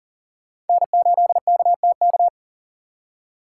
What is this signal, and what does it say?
Morse code, heard as a keyed tone, about 52 wpm, 690 Hz. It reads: N8KTK